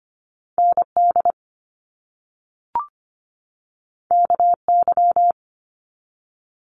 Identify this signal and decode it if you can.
Morse code, heard as a keyed tone, about 25 wpm, 700 Hz. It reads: ND KY
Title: US States 25wpm